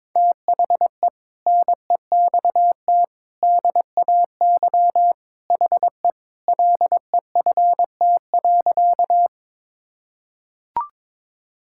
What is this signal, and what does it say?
Morse code, heard as a keyed tone, about 22 wpm, 700 Hz. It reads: THE NEXT DAY HE LEFT.
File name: practice / 22wpm / Sentences with 3-Word Phrases - 22wpm